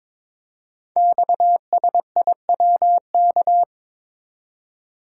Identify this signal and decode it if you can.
Morse code, heard as a keyed tone, about 22 wpm, 700 Hz. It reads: XSIWK
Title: Code Groups - 5 Letters - 22wpm